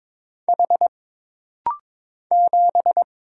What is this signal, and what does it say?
Morse code, heard as a keyed tone, about 22 wpm, 700 Hz. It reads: H 7
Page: This rapid-fire timing is geared to push your ICR (Instant Character Recognition) skills.